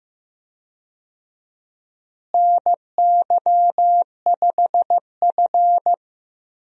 Morse code, heard as a keyed tone, about 14 wpm, 700 Hz. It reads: NY5F
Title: USA Call Signs - 4 Characters - Repeated 3x - 15wpm